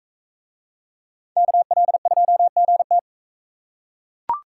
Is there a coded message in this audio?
Yes: Morse code from 1.4 to 3.0 s, heard as a keyed tone, about 40 wpm, 700 Hz. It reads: KL2GT